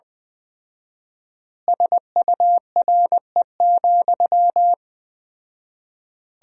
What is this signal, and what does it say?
Morse code, heard as a keyed tone, about 20 wpm, 700 Hz. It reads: SURE,
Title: 4x Word-Spacing - Sentences from Top 500 Words - 20wpm